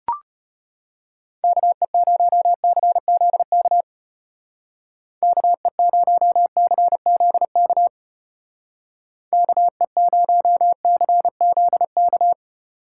Morse code, heard as a keyed tone, about 32 wpm, 700 Hz. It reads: KE0CZK KE0CZK KE0CZK